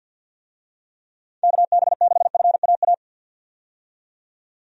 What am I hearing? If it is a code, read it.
Morse code, heard as a keyed tone, about 51 wpm, 700 Hz. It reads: KB6FAA
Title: USA Call Signs - 6 Characters - 50wpm